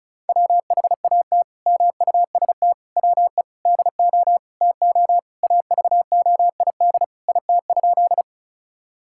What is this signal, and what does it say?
Morse code, heard as a keyed tone, about 34 wpm, 700 Hz. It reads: WHAT MUST WE DO TO AVOID IT?
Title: Sentences from Top 1000 Words - Review Part 6 - 35wpm